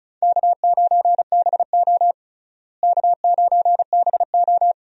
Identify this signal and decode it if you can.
Morse code, heard as a keyed tone, about 36 wpm, 700 Hz. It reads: K9BO K9BO
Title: USA Call Signs - 4 Characters - Repeated 2x - 35wpm